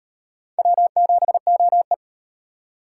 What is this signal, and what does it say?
Morse code, heard as a keyed tone, about 38 wpm, 700 Hz. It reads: W7OE